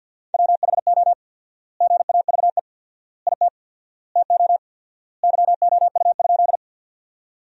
Morse code, heard as a keyed tone, about 50 wpm, 700 Hz. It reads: WHO GAVE IT TO YOU?
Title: 4x Word-Spacing - Sentences from Top 500 Words - 50wpm